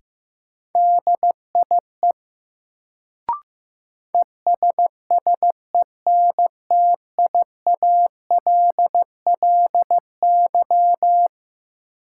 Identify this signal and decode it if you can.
Morse code, heard as a keyed tone, about 15 wpm, 700 Hz. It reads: DIE ESSENTIALLY
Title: Sentences from Top 1300 Words - New Word Drill Alt - 15wpm